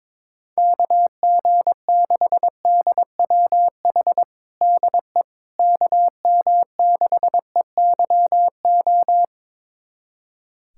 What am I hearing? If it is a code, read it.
Morse code, heard as a keyed tone, about 22 wpm, 700 Hz. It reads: KG6DWH DE KM6EYO
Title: USA Callsign DE Callsign - 6 Characters - 22wpm